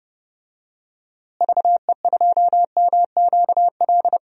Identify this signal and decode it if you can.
Morse code, heard as a keyed tone, about 32 wpm, 700 Hz. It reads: VE2MQL
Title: DXCC-100 Call Signs - 6 Characters - Repeated 3x - 30wpm